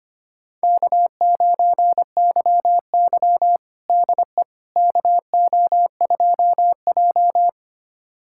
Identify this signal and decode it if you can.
Morse code, heard as a keyed tone, about 25 wpm, 700 Hz. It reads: K9YY DE KO2J
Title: USA Callsign DE Callsign - 4 Characters - 25wpm